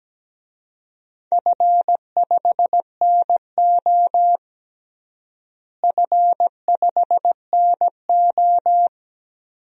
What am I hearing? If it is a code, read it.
Morse code, heard as a keyed tone, about 17 wpm, 700 Hz. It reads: F5NO F5NO
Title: DXCC-100 Call Signs - 4 Characters - Repeated 2x - 17wpm